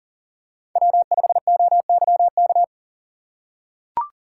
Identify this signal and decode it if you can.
Morse code, heard as a keyed tone, about 40 wpm, 700 Hz. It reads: W5OYK
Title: USA Call Signs - 6x Character Spacing - 40wpm